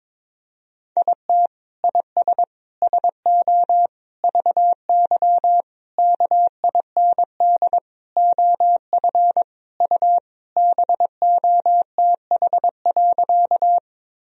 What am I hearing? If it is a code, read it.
Morse code, heard as a keyed tone, about 22 wpm, 700 Hz. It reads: IT IS SO VY KIND OF U BOTH.
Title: Sentences from Top 300 Words and Abbreviations 22wpm